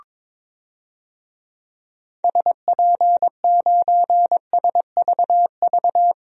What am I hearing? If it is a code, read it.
Morse code, heard as a keyed tone, about 22 wpm, 700 Hz. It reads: SP9SVV
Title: DXCC-100 Call Signs - 6 Characters - 22wpm